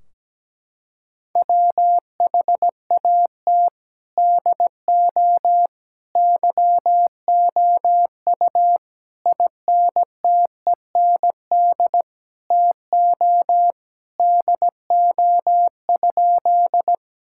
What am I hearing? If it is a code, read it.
Morse code, heard as a keyed tone, about 17 wpm, 700 Hz. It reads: WHAT DO YOU INTEND TO DO?
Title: Sentences from Top 1600 Words 17wpm